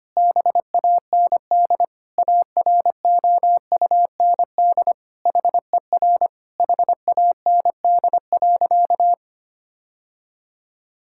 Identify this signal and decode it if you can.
Morse code, heard as a keyed tone, about 25 wpm, 700 Hz. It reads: BAND AROUND HER HAND.
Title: Sentences from Top 1600 Words 25wpm